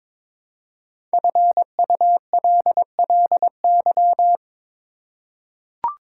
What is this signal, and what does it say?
Morse code, heard as a keyed tone, about 22 wpm, 700 Hz. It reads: FULLY
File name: 5 Letter Words - First 3 Letters Given - 22wpm